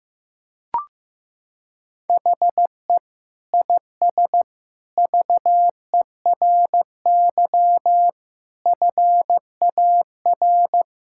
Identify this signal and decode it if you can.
Morse code, heard as a keyed tone, about 15 wpm, 700 Hz. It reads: HE IS VERY FAR
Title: Sentences from Top 1000 Words - Review Part 6 - 15wpm